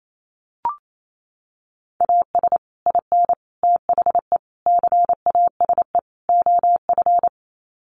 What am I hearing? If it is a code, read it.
Morse code, heard as a keyed tone, about 28 wpm, 700 Hz. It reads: AS IN THE CASE OF
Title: Sets of 5 Words 28wpm